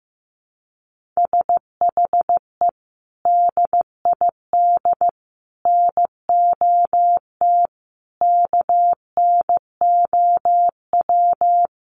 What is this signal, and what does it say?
Morse code, heard as a keyed tone, about 14 wpm, 700 Hz. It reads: SHE DID NOT KNOW
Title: Sets of 5 Words Encore2 15wpm